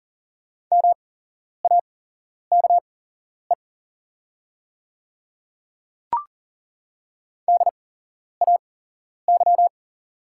Morse code, heard as a keyed tone, about 40 wpm, spaced as for 11 wpm, 690 Hz. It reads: MAKE DAY
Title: 8x Character Spacing - Top 100 Words - 40wpm